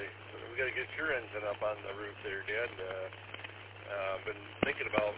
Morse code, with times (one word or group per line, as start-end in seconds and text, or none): none